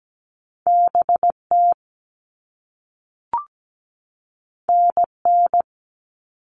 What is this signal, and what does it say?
Morse code, heard as a keyed tone, about 17 wpm, 700 Hz. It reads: BT NN
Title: All Cut Numbers - 2-Digit 17wpm